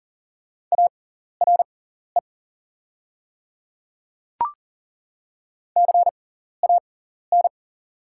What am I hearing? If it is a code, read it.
Morse code, heard as a keyed tone, about 40 wpm, spaced as for 14 wpm, 700 Hz. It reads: ARE CAN
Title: 6x Character Spacing - Top 200 Words - 40wpm